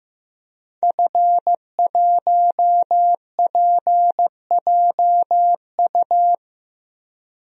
Morse code, heard as a keyed tone, about 15 wpm, 700 Hz. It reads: F1PJU
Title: DXCC-100 Call Signs - 5 Characters - Repeated 2x - 15wpm